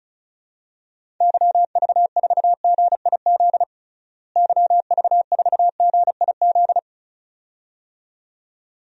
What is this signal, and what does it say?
Morse code, heard as a keyed tone, about 36 wpm, 700 Hz. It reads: YV4GIZ YV4GIZ
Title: DXCC-100 Call Signs - 6 Characters - Repeated 2x - 35wpm